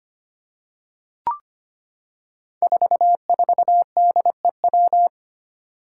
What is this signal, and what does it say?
Morse code, heard as a keyed tone, about 25 wpm, 700 Hz. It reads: 44DEW
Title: Five Challenging Letter-Number Combinations 25wpm